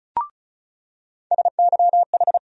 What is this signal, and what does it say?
Morse code, heard as a keyed tone, about 34 wpm, 700 Hz. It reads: SYH
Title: USA Call Sign Postfixes - 3 Letters 35wpm